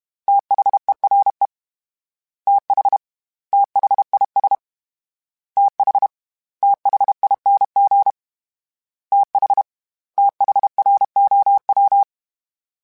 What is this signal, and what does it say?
Morse code, heard as a keyed tone, about 32 wpm, 800 Hz. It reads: THERE TH THIS TH THING TH THROW
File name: TH / TH mp3s / TH Prefixes 1x1 @30